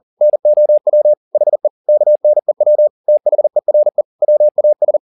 Morse code, heard as a keyed tone, about 40 wpm, 600 Hz. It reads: NOW HE KNEW THERE WAS
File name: the old man and the sea 2 40wpm